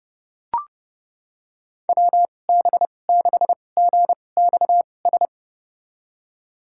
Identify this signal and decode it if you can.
Morse code, heard as a keyed tone, about 31 wpm, spaced as for 22 wpm, 700 Hz. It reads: WB6GXS
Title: USA Call Signs - 6 Characters - 2x Character Spacing - 30wpm